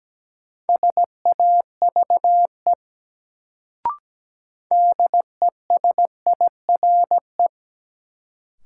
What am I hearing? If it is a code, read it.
Morse code, heard as a keyed tone, about 17 wpm, 700 Hz. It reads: SAVE DESIRE
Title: Sentences from Top 1100 Words - New Word Drill Alt - 17wpm